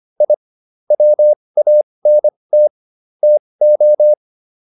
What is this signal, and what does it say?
Morse code, heard as a keyed tone, about 25 wpm, spaced as for 20 wpm, 600 Hz. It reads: I WANT TO
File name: WD202 20